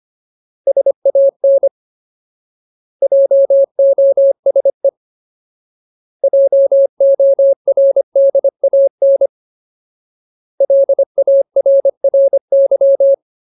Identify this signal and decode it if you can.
Morse code, heard as a keyed tone, about 25 wpm, 550 Hz. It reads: SAN JOSE JORDAN LARRY
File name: Ultimate Word List / 25WPM Ultimate